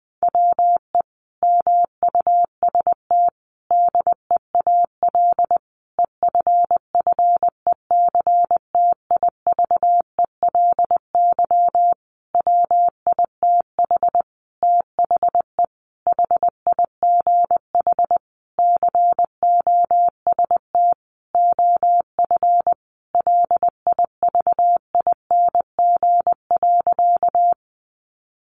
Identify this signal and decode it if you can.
Morse code, heard as a keyed tone, about 20 wpm, 700 Hz. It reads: WE MUST DEAL EFFECTIVELY WITH THE HIGH COST OF LIVING.
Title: Sentences from Top 1400 Words 20wpm